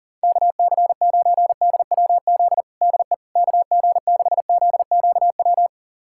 Morse code, heard as a keyed tone, about 37 wpm, 700 Hz. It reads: KC9DWZ DE KG6ZQW
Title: USA Callsign DE Callsign - 6 Characters - 40wpm